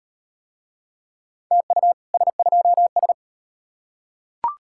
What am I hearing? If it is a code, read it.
Morse code, heard as a keyed tone, about 34 wpm, 700 Hz. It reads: TU S2S